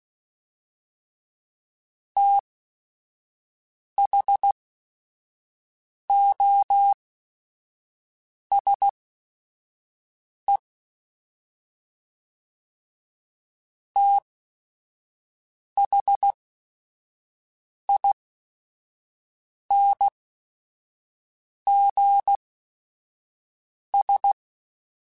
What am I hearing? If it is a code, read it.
Morse code, heard as a keyed tone, about 16 wpm, spaced as for 5 wpm, 800 Hz. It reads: THOSE THINGS